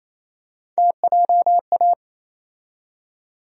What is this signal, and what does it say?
Morse code, heard as a keyed tone, about 28 wpm, 700 Hz. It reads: TJA
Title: USA Call Sign Postfixes - 3 Letters 28wpm